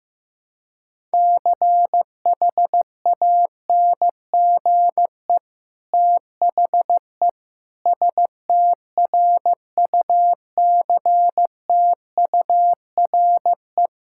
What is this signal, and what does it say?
Morse code, heard as a keyed tone, about 15 wpm, 700 Hz. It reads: CHANGE THE STRUCTURE
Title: Sentences from Top 1000 Words - Review Part 6 - 15wpm